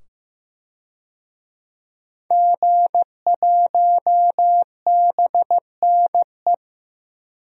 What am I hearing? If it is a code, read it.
Morse code, heard as a keyed tone, about 16 wpm, 700 Hz. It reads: G1BNE
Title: DXCC-100 Call Signs - 5 Characters - 15wpm